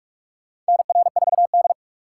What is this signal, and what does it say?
Morse code, heard as a keyed tone, about 43 wpm, 700 Hz. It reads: NR4D
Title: USA Call Signs - 4 Characters - 3x Character Spacing - 45wpm